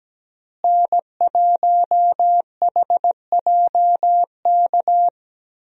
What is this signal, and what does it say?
Morse code, heard as a keyed tone, about 17 wpm, 700 Hz. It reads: N1HJK
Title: USA Call Signs - 5 Characters - Repeated 2x - 17wpm